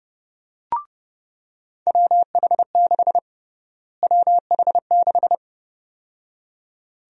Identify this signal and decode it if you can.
Morse code, heard as a keyed tone, about 31 wpm, 700 Hz. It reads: WH6 WH6
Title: USA Call Sign Prefixes - Repeated 2x - 30wpm